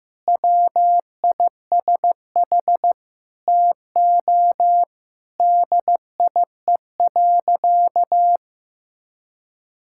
Morse code, heard as a keyed tone, about 15 wpm, 700 Hz. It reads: WISH TO DIE.
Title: Sentences from Top 1600 Words 15wpm